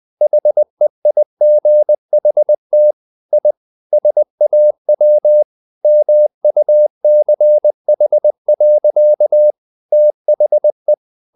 Morse code, heard as a keyed tone, about 20 wpm, 600 Hz. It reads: HEIGHT I SAW MUCH. THE
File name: the old man and the sea part 2 20wpm